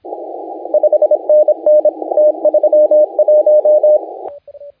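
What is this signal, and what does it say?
Morse code, heard as a keyed tone, about 27 wpm, 560 Hz, with radio noise. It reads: 5NN T31
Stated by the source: Morse contest audio recorded by LA4C during the SAC CW 2018 contest.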